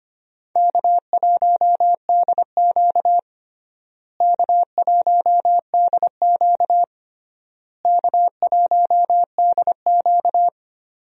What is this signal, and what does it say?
Morse code, heard as a keyed tone, about 25 wpm, 700 Hz. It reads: K1DQ K1DQ K1DQ
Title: USA Call Signs - 4 Characters - Repeated 3x - 25wpm